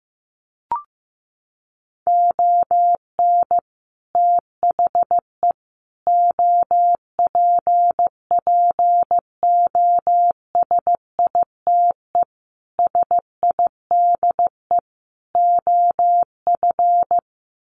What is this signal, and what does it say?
Morse code, heard as a keyed tone, about 15 wpm, 700 Hz. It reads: ON THE OPPOSITE SIDE OF
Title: Sets of 5 Words Encore2 15wpm